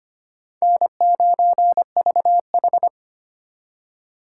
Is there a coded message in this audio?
Yes: Morse code from 0.6 to 2.9 s, heard as a keyed tone, about 32 wpm, 700 Hz. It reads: N9VH